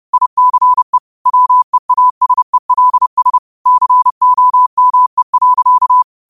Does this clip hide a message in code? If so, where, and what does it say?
Yes: Morse code from 0.1 to 6.0 s, heard as a keyed tone, about 30 wpm, 1000 Hz. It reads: ICE WEASELS COME.